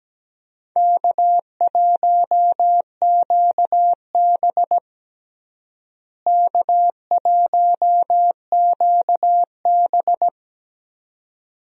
Morse code, heard as a keyed tone, about 17 wpm, 700 Hz. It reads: K1QB K1QB
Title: USA Call Signs - 4 Characters - Repeated 2x - 17wpm